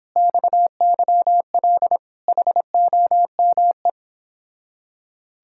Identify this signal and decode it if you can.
Morse code, heard as a keyed tone, about 26 wpm, 700 Hz. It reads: XYL HOME